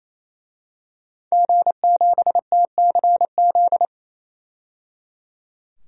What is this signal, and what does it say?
Morse code, heard as a keyed tone, about 28 wpm, 700 Hz. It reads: G7TCZ
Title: DXCC-100 Call Signs - 5 Characters - 28wpm